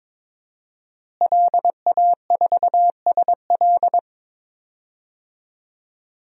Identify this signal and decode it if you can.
Morse code, heard as a keyed tone, about 22 wpm, 700 Hz. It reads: LA4SL
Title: DXCC-100 Call Signs - 5 Characters - Repeated 3x - 22wpm